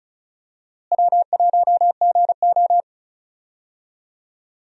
Morse code, heard as a keyed tone, about 34 wpm, 700 Hz. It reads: W1GO
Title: USA Call Signs - 4 Characters - Repeated 3x - 35wpm